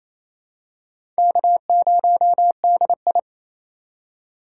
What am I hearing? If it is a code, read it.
Morse code, heard as a keyed tone, about 28 wpm, 700 Hz. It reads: K0DI
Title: USA Call Signs - 4 Characters - 4x Character Spacing - 28wpm